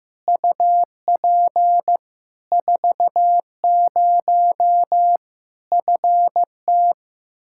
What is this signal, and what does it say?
Morse code, heard as a keyed tone, about 15 wpm, 700 Hz. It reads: UP 40 FT
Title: Practice QSO - Exchange 2 - Info - 15wpm